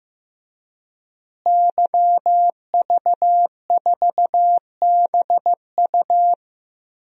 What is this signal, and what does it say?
Morse code, heard as a keyed tone, about 15 wpm, 700 Hz. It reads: YV4BU
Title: DXCC-100 Call Signs - 5 Characters - Repeated 3x - 15wpm